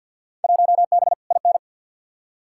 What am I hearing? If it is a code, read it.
Morse code, heard as a keyed tone, about 51 wpm, 700 Hz. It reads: 1B IN
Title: Field Day Exchanges Encore - 50wpm